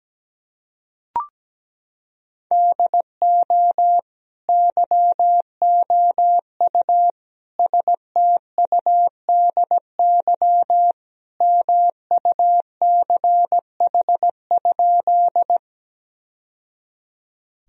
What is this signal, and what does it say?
Morse code, heard as a keyed tone, about 17 wpm, 700 Hz. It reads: DO YOU STUDY MUCH?
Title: Sentences from Top 1000 Words - Review Part 1 - 17wpm